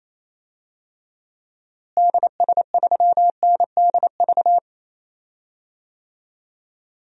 Morse code, heard as a keyed tone, about 28 wpm, 700 Hz. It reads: DS3NDV
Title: DXCC-100 Call Signs - 6 Characters - Repeated 2x - 28wpm